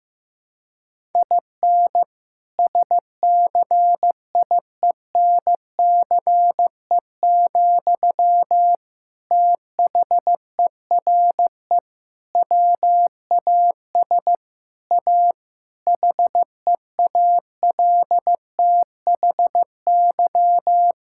Morse code, heard as a keyed tone, about 15 wpm, 700 Hz. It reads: IN SCIENCE, THERE WAS A HEALTHY
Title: Sentences from Top 1000 Words - Review Part 5 - 15wpm